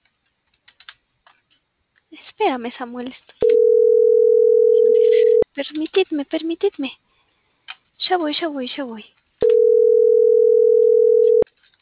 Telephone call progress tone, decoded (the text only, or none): ringback tone